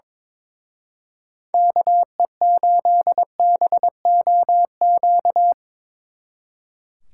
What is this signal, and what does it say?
Morse code, heard as a keyed tone, about 24 wpm, 700 Hz. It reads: KE8BOQ
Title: Speed Racing - USA 2x3 Call Signs 22wpm